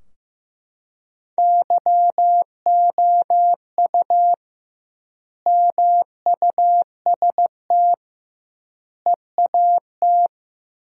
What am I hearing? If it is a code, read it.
Morse code, heard as a keyed tone, about 15 wpm, 700 Hz. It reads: YOU MUST EAT
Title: 2x Word-Spacing - Sentences from Top 300 Words - 15wpm